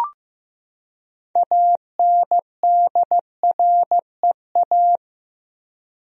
Morse code, heard as a keyed tone, about 15 wpm, 700 Hz. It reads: ANDREA
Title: USA Common Names - 15wpm